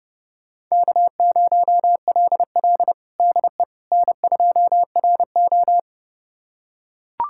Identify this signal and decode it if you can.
Morse code, heard as a keyed tone, about 30 wpm, 700 Hz. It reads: K0LL DE N2RO
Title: USA Callsign DE Callsign - 4 Characters - 30wpm